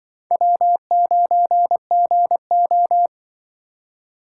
Morse code, heard as a keyed tone, about 24 wpm, 700 Hz. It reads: W9GO
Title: Speed Racing - USA 1x2 Call Signs 20wpm